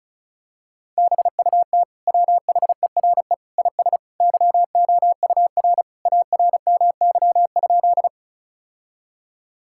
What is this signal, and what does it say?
Morse code, heard as a keyed tone, about 36 wpm, 700 Hz. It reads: BUT WHERE IS YOUR ARMY?
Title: Sentences from Top 1000 Words - Review Part 5 - 35wpm